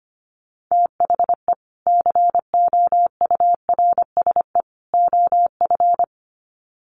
Morse code, heard as a keyed tone, about 25 wpm, 700 Hz. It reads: THE COURSE OF
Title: Sets of 3 Words 25wpm